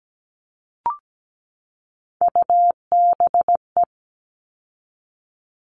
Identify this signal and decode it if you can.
Morse code, heard as a keyed tone, about 17 wpm, 700 Hz. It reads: UBE
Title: All Cut Numbers - 3-Digit 17wpm